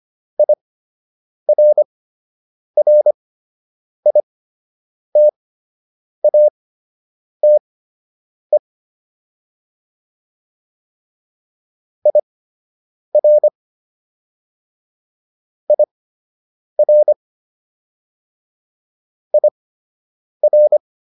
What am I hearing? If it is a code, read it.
Morse code, heard as a keyed tone, about 25 wpm, spaced as for 8 wpm, 600 Hz. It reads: IRRITATE IR IR IR